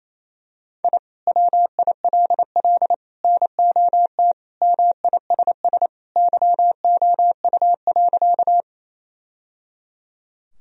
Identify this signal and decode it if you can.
Morse code, heard as a keyed tone, about 28 wpm, 700 Hz. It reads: I WILL NOT MISS YOU.
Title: Sentences from Top 1000 Words - Review Part 3 - 28wpm